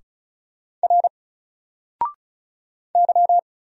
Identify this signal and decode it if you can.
Morse code, heard as a keyed tone, about 36 wpm, 700 Hz. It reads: R Y
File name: Single-Letter-Number-35wpm